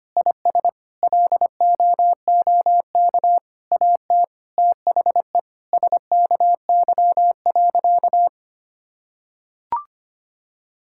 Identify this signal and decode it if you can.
Morse code, heard as a keyed tone, about 25 wpm, 700 Hz. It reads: IS LOOK AT THE SKY.
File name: practice / 25wpm / Sentences with 9-Word Phrases - 25wpm